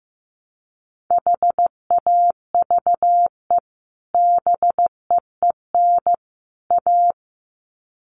Morse code, heard as a keyed tone, about 15 wpm, 700 Hz. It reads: HAVE BEEN A
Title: Sets of 3 Words 15wpm